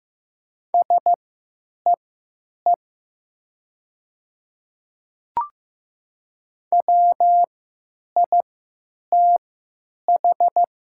Morse code, heard as a keyed tone, about 15 wpm, spaced as for 9 wpm, 700 Hz. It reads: SEE WITH
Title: 3x Character Spacing - Top 100 Words - 15wpm